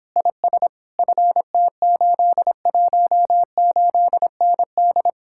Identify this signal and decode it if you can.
Morse code, heard as a keyed tone, about 26 wpm, 700 Hz. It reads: IS FT818ND